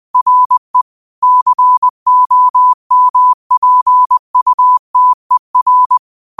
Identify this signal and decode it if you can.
Morse code, heard as a keyed tone, about 20 wpm, 1000 Hz. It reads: RE COMPUTER
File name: 20wpm